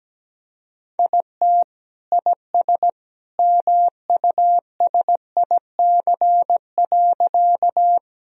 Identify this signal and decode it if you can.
Morse code, heard as a keyed tone, about 17 wpm, 700 Hz. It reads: IT IS MUSIC.
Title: Sentences from Top 1000 Words - Review Part 1 - 17wpm